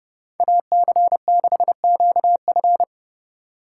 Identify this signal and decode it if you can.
Morse code, heard as a keyed tone, about 29 wpm, 700 Hz. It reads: AC6QF